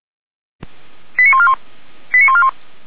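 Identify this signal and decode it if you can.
Five-tone selective call (ZVEI), heard as quick runs of five tones, tones of about 70 ms. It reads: 89141 89141
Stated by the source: ZVEI five-tone selective call received on an SDR.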